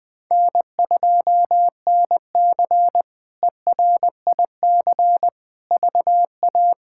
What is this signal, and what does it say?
Morse code, heard as a keyed tone, about 21 wpm, 700 Hz. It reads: N2NC ERIC VA
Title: K1USN SST QSO Practice - 20wpm